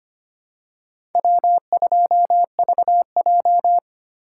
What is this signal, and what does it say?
Morse code, heard as a keyed tone, about 25 wpm, 700 Hz. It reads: W2VJ
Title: USA Call Signs - 4 Characters - 2x Character Spacing - 25wpm